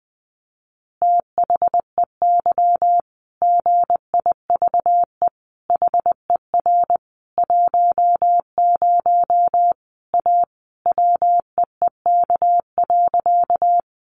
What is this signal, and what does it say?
Morse code, heard as a keyed tone, about 20 wpm, 700 Hz. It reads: THEY GIVE HER 10 A WEEK.
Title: Easy Phrases 20wpm